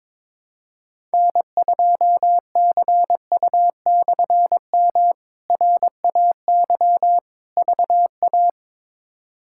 Morse code, heard as a keyed tone, about 21 wpm, 700 Hz. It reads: N2CU/M RAY VA
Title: K1USN SST QSO Practice - 22wpm